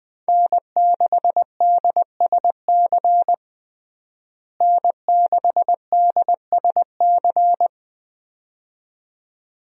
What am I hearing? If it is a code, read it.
Morse code, heard as a keyed tone, about 20 wpm, 700 Hz. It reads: N6DSC N6DSC
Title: USA Call Signs - 5 Characters - Repeated 2x - 20wpm